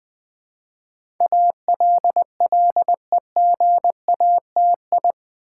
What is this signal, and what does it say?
Morse code, heard as a keyed tone, about 20 wpm, 700 Hz. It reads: ALLEGATI
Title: Build A Word - 4 Letters At A Time - 10-Letter Words - 20wpm